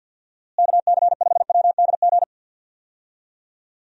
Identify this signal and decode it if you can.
Morse code, heard as a keyed tone, about 52 wpm, 700 Hz. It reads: KC5WDG
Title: USA Call Signs - 6 Characters - Repeated 3x - 50wpm